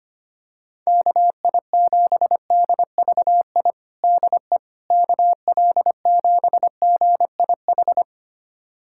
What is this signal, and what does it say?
Morse code, heard as a keyed tone, about 25 wpm, 700 Hz. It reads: KI7DVI DE KL7GIH
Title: USA Callsign DE Callsign - 6 Characters - 25wpm